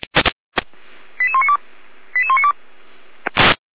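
Five-tone selective call (ZVEI), heard as quick runs of five tones, tones of about 70 ms. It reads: 88192 88192
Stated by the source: Five-tone ZVEI selective call, received with an SDR.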